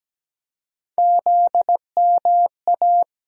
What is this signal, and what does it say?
Morse code, heard as a keyed tone, about 17 wpm, 700 Hz. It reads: ZMA
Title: USA Call Sign Postfixes - 3 Letters 17wpm